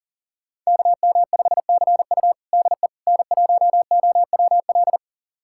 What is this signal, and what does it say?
Morse code, heard as a keyed tone, about 40 wpm, 700 Hz. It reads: KM5CU DE N1OWL
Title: USA Callsign DE Callsign - 5 Characters - 40wpm